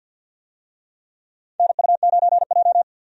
Morse code, heard as a keyed tone, about 53 wpm, 700 Hz. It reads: NU9J
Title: USA Call Signs - 4 Characters - 50wpm